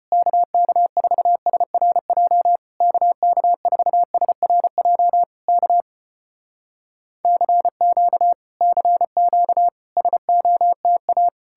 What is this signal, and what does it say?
Morse code, heard as a keyed tone, about 34 wpm, 700 Hz. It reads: KK4SRJ KK4SRJ K CQ CQ SOTA